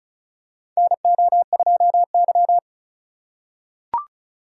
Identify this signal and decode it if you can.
Morse code, heard as a keyed tone, about 34 wpm, 700 Hz. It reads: NO2Y
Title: USA Call Signs - 4 Characters - 4x Character Spacing - 35wpm